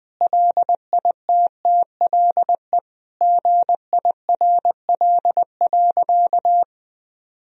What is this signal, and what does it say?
Morse code, heard as a keyed tone, about 20 wpm, 700 Hz. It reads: LITTLE GIRL.
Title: Sentences from Top 1000 Words - Review Part 1 - 20wpm